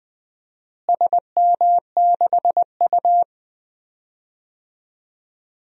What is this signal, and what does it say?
Morse code, heard as a keyed tone, about 21 wpm, 700 Hz. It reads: SM6U
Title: DXCC-100 Call Signs - 4 Characters - Repeated 2x - 20wpm